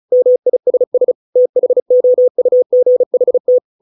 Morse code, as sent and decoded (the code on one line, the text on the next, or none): -- .. ... ... / - .... --- ..- --. .... -
MISS THOUGHT